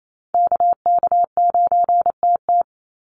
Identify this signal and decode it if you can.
Morse code, heard as a keyed tone, about 28 wpm, 700 Hz. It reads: KK9TT
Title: USA Call Signs 28wpm